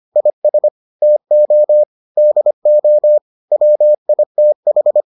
Morse code, heard as a keyed tone, about 25 wpm, 600 Hz. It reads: IS TO DO WITH